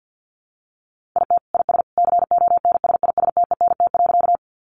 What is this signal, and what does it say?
Morse code, heard as a keyed tone, about 50 wpm, 700 Hz. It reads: IT IS CONSISTENT.
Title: Sentences from Top 1500 Words 50wpm